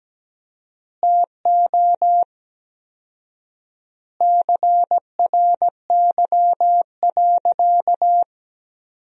Morse code, heard as a keyed tone, about 17 wpm, 700 Hz. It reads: TO CRY.
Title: 4x Word-Spacing - Sentences from Top 500 Words - 17wpm